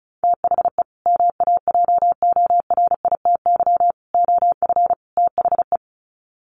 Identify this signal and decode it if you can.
Morse code, heard as a keyed tone, about 34 wpm, 700 Hz. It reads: THE MAJORITY OF THE
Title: Sets of 4 Words 35wpm